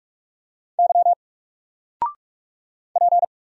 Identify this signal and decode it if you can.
Morse code, heard as a keyed tone, about 46 wpm, 700 Hz. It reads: Y P
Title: Single Character (Letter:Number:Punctuation:Prosign) 45wpm